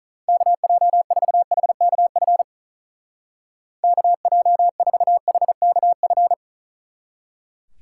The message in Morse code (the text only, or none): KJ4HKF KJ4HKF